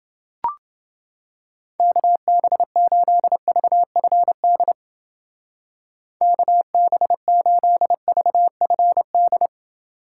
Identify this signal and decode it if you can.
Morse code, heard as a keyed tone, about 28 wpm, 700 Hz. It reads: KB8VFD KB8VFD